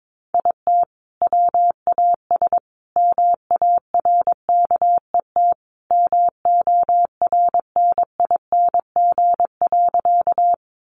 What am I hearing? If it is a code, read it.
Morse code, heard as a keyed tone, about 22 wpm, 700 Hz. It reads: IT WAS MARKET MORNING.
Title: Sentences from Top 600 Words 22wpm